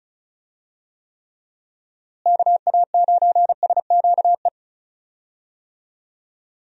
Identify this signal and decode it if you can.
Morse code, heard as a keyed tone, about 36 wpm, 700 Hz. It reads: KA9SQE
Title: USA Call Signs - 6 Characters - Repeated 2x - 35wpm